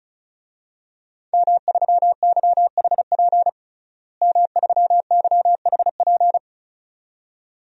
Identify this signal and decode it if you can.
Morse code, heard as a keyed tone, about 36 wpm, 700 Hz. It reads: M3YHP M3YHP
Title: DXCC-100 Call Signs - 5 Characters - Repeated 2x - 35wpm